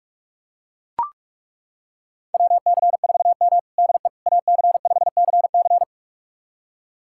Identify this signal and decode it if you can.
Morse code, heard as a keyed tone, about 42 wpm, 700 Hz. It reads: WC4M DE AC5CC
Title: USA Callsign DE Callsign 45wpm